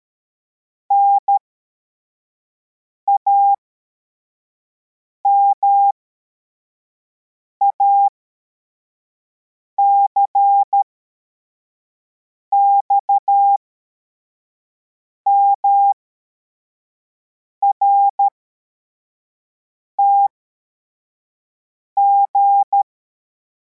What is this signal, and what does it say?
Morse code, heard as a keyed tone, about 13 wpm, spaced as for 4 wpm, 800 Hz. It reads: NAMAC XMRTG
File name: formation / cw / Text33